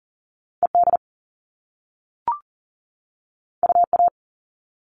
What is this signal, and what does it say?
Morse code, heard as a keyed tone, about 41 wpm, 700 Hz. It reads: ED UA
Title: All Cut Numbers - 2-Digit 40wpm